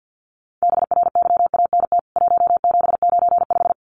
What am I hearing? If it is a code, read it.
Morse code, heard as a keyed tone, about 48 wpm, 700 Hz. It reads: BRYANT 1795
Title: CWT Exchange 50wpm